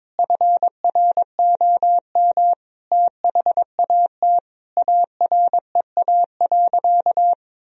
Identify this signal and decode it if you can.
Morse code, heard as a keyed tone, about 21 wpm, 700 Hz. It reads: FROM THAT AREA.
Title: Sentences from Top 1000 Words - Review Part 2 - 22wpm